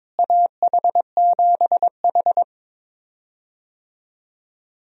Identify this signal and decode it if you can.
Morse code, heard as a keyed tone, about 22 wpm, 700 Hz. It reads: AH7H
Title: USA Call Signs - 4 Characters - Repeated 3x - 22wpm